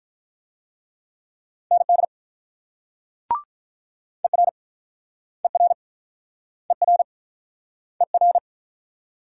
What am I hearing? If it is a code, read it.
Morse code, heard as a keyed tone, about 50 wpm, 700 Hz. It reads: ND ER ER ER ER